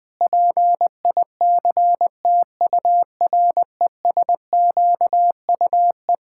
Morse code, heard as a keyed tone, about 20 wpm, 700 Hz. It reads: PICTURESQUE
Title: Pangram Challenge v2 - 20wpm